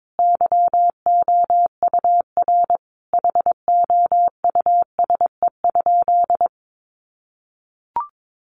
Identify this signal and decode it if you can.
Morse code, heard as a keyed tone, about 22 wpm, 700 Hz. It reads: YOUR HOUSE?
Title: Sentences from Top 1500 Words 22wpm